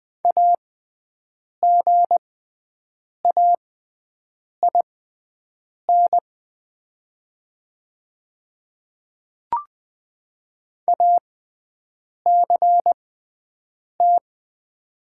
Morse code, heard as a keyed tone, about 20 wpm, spaced as for 7 wpm, 700 Hz. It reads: AGAIN ACT
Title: 6x Character Spacing - Top 200 Words - 20wpm